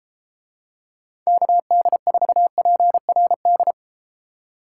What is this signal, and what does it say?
Morse code, heard as a keyed tone, about 33 wpm, 700 Hz. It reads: KD4PRD